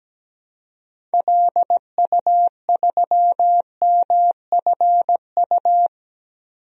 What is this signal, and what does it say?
Morse code, heard as a keyed tone, about 17 wpm, 700 Hz. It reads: LU3MFU
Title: DXCC-100 Call Signs - 6 Characters - Repeated 2x - 17wpm